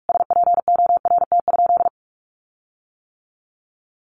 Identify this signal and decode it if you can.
Morse code, heard as a keyed tone, about 45 wpm, 700 Hz. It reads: SPORT?
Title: Sentences from Top 1500 Words 45wpm